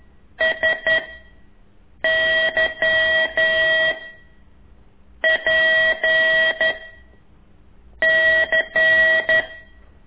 Morse code, heard as a keyed tone, about 11 wpm, 1950 Hz. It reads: S Y P C
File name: ik1weg2 / CORSI CW